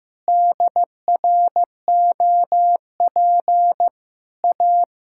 Morse code, heard as a keyed tone, about 15 wpm, 700 Hz. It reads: DROP A
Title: Pangram Challenge v2 - 15wpm